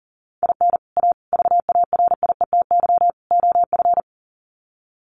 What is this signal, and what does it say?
Morse code, heard as a keyed tone, about 40 wpm, 700 Hz. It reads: IN A VARIETY OF
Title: Sets of 4 Words 40wpm